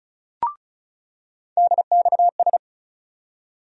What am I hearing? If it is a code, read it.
Morse code, heard as a keyed tone, about 36 wpm, 700 Hz. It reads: DXS
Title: USA Call Sign Postfixes - 3 Letters 35wpm